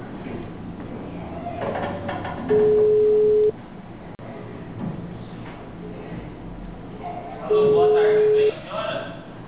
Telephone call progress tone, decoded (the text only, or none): ringback tone